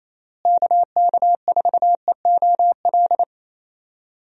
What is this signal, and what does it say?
Morse code, heard as a keyed tone, about 28 wpm, 700 Hz. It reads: KK4EOL
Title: USA Call Signs - 6 Characters - Repeated 3x - 28wpm